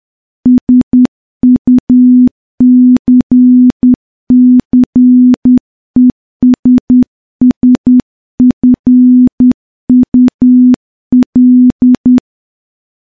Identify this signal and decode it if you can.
Morse code, heard as a keyed tone, about 10 wpm, 250 Hz. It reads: SUCCESSFUL
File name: morse-training-10WPM